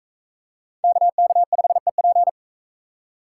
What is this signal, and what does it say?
Morse code, heard as a keyed tone, about 43 wpm, 700 Hz. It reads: KK5EP